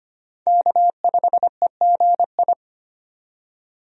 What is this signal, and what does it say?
Morse code, heard as a keyed tone, about 25 wpm, 690 Hz. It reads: K5EGI
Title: USA Call Signs - 5 Characters - 25wpm